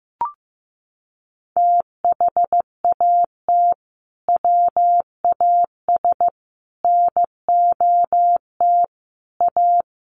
Morse code, heard as a keyed tone, about 15 wpm, 700 Hz. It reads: THAT WAS NOT A
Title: Sentences from Top 1200 Words 15wpm